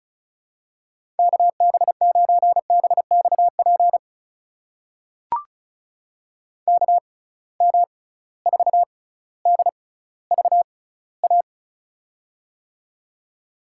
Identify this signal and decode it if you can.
Morse code, heard as a keyed tone, about 34 wpm, spaced as for 18 wpm, 700 Hz. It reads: KB9BXP KM4DVA
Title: USA Call Signs - 6 Characters - 6x Character Spacing - 35wpm